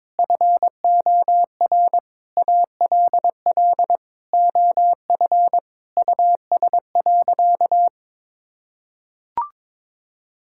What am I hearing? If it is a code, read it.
Morse code, heard as a keyed tone, about 22 wpm, 700 Hz. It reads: FOR ALL OF US.
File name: practice / 22wpm / Sentences with 4-Word Phrases - 22wpm